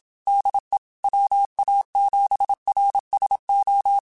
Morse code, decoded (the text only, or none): DE WA7RSO